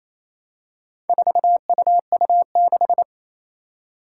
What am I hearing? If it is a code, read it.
Morse code, heard as a keyed tone, about 27 wpm, 700 Hz. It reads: 4UU6
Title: Four Challenging Letter-Number Combinations 28wpm